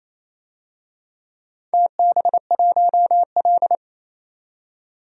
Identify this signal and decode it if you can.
Morse code, heard as a keyed tone, about 28 wpm, 700 Hz. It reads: TB1L
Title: DXCC-100 Call Signs - 4 Characters - 28wpm